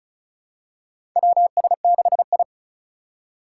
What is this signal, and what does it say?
Morse code, heard as a keyed tone, about 34 wpm, 700 Hz. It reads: WS6I
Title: USA Call Signs - 4 Characters - 3x Character Spacing - 35wpm